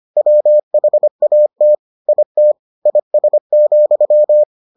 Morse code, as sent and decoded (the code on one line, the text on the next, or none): .-- .... .- - / .. - / .. ... --..--
WHAT IT IS,